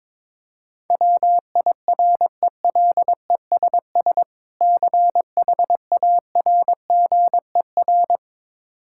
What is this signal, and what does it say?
Morse code, heard as a keyed tone, about 22 wpm, 700 Hz. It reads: WIRELESS CHARGER
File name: Difficult Words with Hints - 22wpm